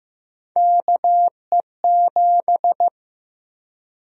Morse code, heard as a keyed tone, about 15 wpm, 700 Hz. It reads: KE7
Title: USA Call Sign Prefixes - Repeated 2x - 15wpm